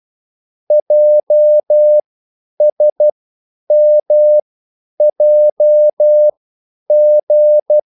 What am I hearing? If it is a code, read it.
Morse code, heard as a keyed tone, about 12 wpm, spaced as for 9 wpm, 600 Hz. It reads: JSMJG